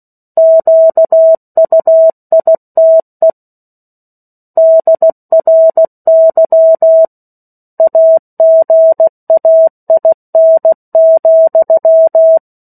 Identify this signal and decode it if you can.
Morse code, heard as a keyed tone, about 16 wpm, 650 Hz. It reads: QUITE DRY AGAIN,
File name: 03-Chapter-03 16 WPM